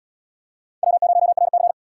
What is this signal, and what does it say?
Morse code, heard as a keyed tone, about 74 wpm, 700 Hz. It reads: W0RC